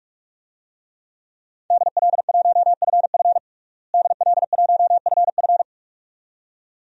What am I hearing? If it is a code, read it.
Morse code, heard as a keyed tone, about 46 wpm, 700 Hz. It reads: DL1FF DL1FF
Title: DXCC-100 Call Signs - 5 Characters - Repeated 2x - 45wpm